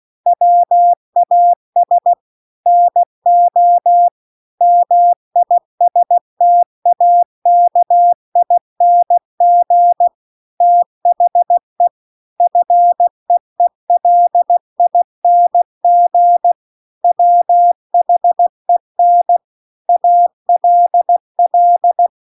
Morse code, heard as a keyed tone, about 16 wpm, 700 Hz. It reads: WAS NO MISTAKING THE FEELING WHEN ALL
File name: CW-The-Big-Twoheaded-River-2-16WPM